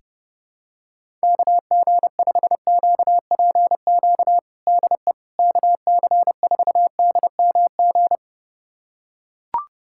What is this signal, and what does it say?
Morse code, heard as a keyed tone, about 30 wpm, 700 Hz. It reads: KG5QPQ DE KC4DMG
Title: USA Callsign DE Callsign - 6 Characters - 30wpm